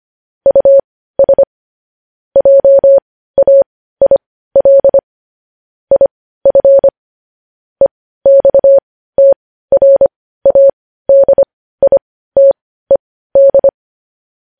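Morse code, heard as a keyed tone, about 25 wpm, spaced as for 15 wpm, 550 Hz. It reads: US JAIL IF EXTRADITED
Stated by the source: News headlines sent in Morse code at 15 wpm.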